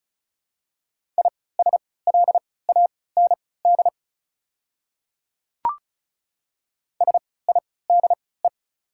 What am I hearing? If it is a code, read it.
Morse code, heard as a keyed tone, about 33 wpm, spaced as for 20 wpm, 690 Hz. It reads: ISLAND SIDE
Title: 3x Character Spacing - Top 500 Words - 35wpm